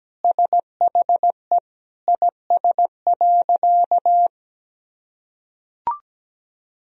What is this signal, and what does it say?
Morse code, heard as a keyed tone, about 17 wpm, 700 Hz. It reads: SHE IS.
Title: Sentences from Top 1000 Words - Review Part 4 - 17wpm